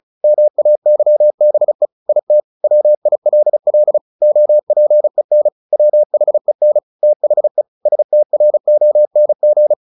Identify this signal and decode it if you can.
Morse code, heard as a keyed tone, about 34 wpm, 600 Hz. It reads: MAYBE IT WILL OPEN WHEN THE STRONG
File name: the old man and the sea / the old man and the sea 1 35wpm